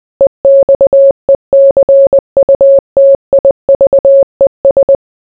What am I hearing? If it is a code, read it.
Morse code, heard as a keyed tone, about 20 wpm, 550 Hz. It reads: EXECUTIVES